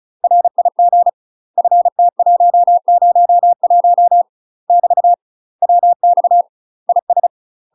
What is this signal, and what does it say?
Morse code, heard as a keyed tone, about 36 wpm, 700 Hz. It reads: RIG FT101 <BT> WX IS
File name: UK MASTER QSOs TRAINING / 35WPM / UK MASTER QSOs 0023